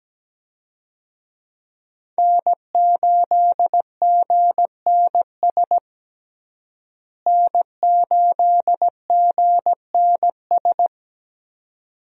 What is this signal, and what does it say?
Morse code, heard as a keyed tone, about 17 wpm, 700 Hz. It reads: N8GNS N8GNS
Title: USA Call Signs - 5 Characters - Repeated 2x - 17wpm